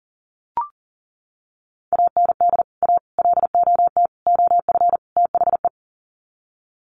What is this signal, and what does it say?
Morse code, heard as a keyed tone, about 40 wpm, 700 Hz. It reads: AND A LOT OF THE
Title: Sets of 5 Words Encore 40wpm